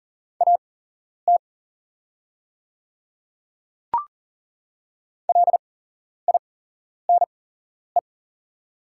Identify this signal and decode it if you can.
Morse code, heard as a keyed tone, about 40 wpm, spaced as for 11 wpm, 700 Hz. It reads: AT LINE
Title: 8x Character Spacing - Top 200 Words - 40wpm